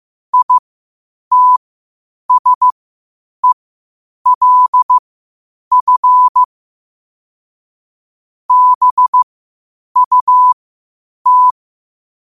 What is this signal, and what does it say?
Morse code, heard as a keyed tone, about 15 wpm, spaced as for 8 wpm, 1000 Hz. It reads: ITSELF BUT
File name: qotdfiles / 8wpm